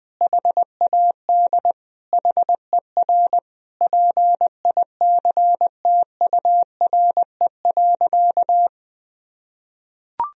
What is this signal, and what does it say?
Morse code, heard as a keyed tone, about 21 wpm, 700 Hz. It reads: HAD HER PICTURE.
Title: Sentences from Top 1000 Words - Review Part 1 - 20wpm